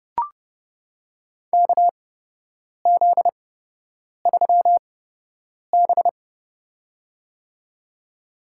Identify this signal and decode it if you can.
Morse code, heard as a keyed tone, about 29 wpm, spaced as for 8 wpm, 700 Hz. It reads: KZ3B
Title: USA Call Signs - 4 Characters - 8x Character Spacing - 30wpm